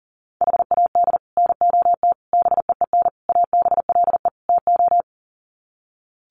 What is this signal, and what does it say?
Morse code, heard as a keyed tone, about 40 wpm, 700 Hz. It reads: HAD NOT BEEN ABLE TO
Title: Sets of 5 Words Encore 40wpm